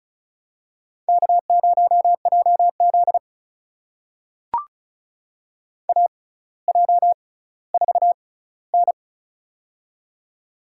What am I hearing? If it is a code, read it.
Morse code, heard as a keyed tone, about 36 wpm, spaced as for 18 wpm, 700 Hz. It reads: K0JZ AJ4N
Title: USA Call Signs - 4 Characters - 6x Character Spacing - 35wpm